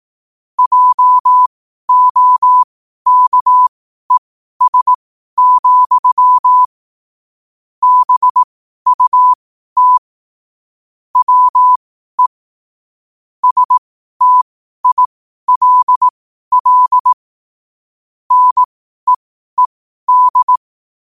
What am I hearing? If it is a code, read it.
Morse code, heard as a keyed tone, about 18 wpm, spaced as for 12 wpm, 1000 Hz. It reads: JOKES, BUT WE STILL NEED
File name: qotdfiles / 12wpm